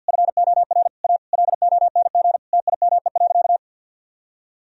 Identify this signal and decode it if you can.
Morse code, heard as a keyed tone, about 51 wpm, 700 Hz. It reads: FOR A LONG TIME.
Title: Sentences from Top 1000 Words - Review Part 1 - 50wpm